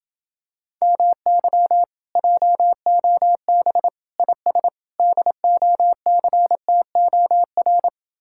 Morse code, heard as a keyed tone, about 28 wpm, 700 Hz. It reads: MY JOB IS DOCTOR